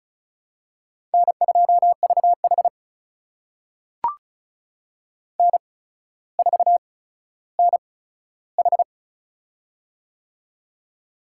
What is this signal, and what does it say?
Morse code, heard as a keyed tone, about 34 wpm, spaced as for 15 wpm, 700 Hz. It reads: N2VH N4NH
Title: USA Call Signs - 4 Characters - 8x Character Spacing - 35wpm